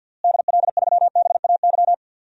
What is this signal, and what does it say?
Morse code, heard as a keyed tone, about 52 wpm, 700 Hz. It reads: DL3BAY
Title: DXCC-100 Call Signs - 6 Characters - Repeated 3x - 50wpm